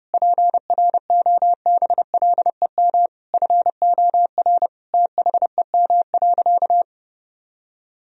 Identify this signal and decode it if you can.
Morse code, heard as a keyed tone, about 32 wpm, 700 Hz. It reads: PROBLEM FOR THEM.
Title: Sentences from Top 1000 Words - Review Part 2 - 30wpm